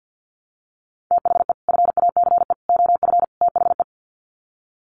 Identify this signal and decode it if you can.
Morse code, heard as a keyed tone, about 50 wpm, 700 Hz. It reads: THE FACE OF THE
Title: Sets of 4 Words 50wpm